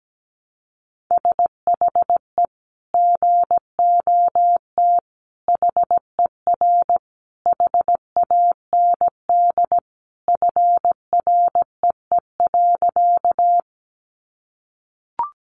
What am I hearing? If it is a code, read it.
Morse code, heard as a keyed tone, about 17 wpm, 700 Hz. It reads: SHE GOT HER HAND FREE.
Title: Sentences from Top 500 Words 17wpm